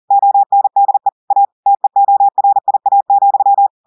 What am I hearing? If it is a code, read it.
Morse code, heard as a keyed tone, about 40 wpm, 800 Hz. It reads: ONDE A TEORIA,
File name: audios / portugais / portugais texte en clair 1 vitesse 40 mots minutes